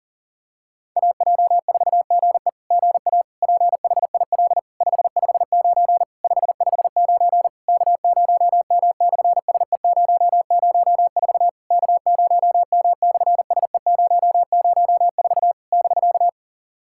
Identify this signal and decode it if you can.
Morse code, heard as a keyed tone, about 38 wpm, 700 Hz. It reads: AJ4GE GA PHIL 559 559 K0M/SE004 K0M/SE004 <BK>
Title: SOTA QSO Practice 40wpm